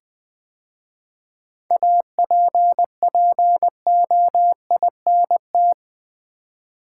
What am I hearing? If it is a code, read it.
Morse code, heard as a keyed tone, about 20 wpm, 700 Hz. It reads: APPOINT
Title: 7 Letter Words - First 3 Letters Given - 20wpm